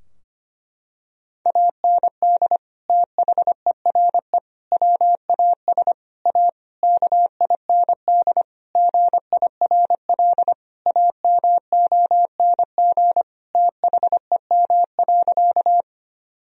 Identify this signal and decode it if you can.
Morse code, heard as a keyed tone, about 25 wpm, 700 Hz. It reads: AND THERE WAS A KIND GIRL AMONG THEM.
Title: Sentences from Top 1000 Words - Review Part 3 - 25wpm